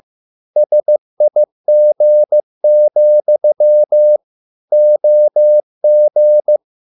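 Morse code, heard as a keyed tone, about 15 wpm, 600 Hz. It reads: SIG, OG
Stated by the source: Morse code at a slow 15 wpm from CW med Gnister.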